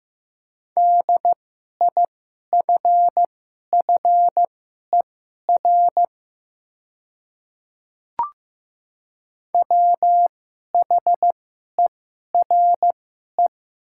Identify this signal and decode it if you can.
Morse code, heard as a keyed tone, about 15 wpm, spaced as for 11 wpm, 700 Hz. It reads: DIFFER WHERE
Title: 2x Character Spacing - Top 300 Words - 15wpm